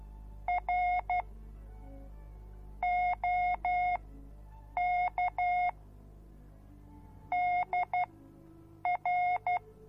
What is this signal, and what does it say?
Morse code, heard as a keyed tone, about 12 wpm, spaced as for 7 wpm, 720 Hz. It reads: R OK DR